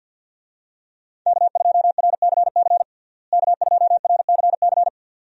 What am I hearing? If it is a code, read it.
Morse code, heard as a keyed tone, about 52 wpm, 700 Hz. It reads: K2RCC K2RCC
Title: USA Call Signs - 5 Characters - Repeated 2x - 50wpm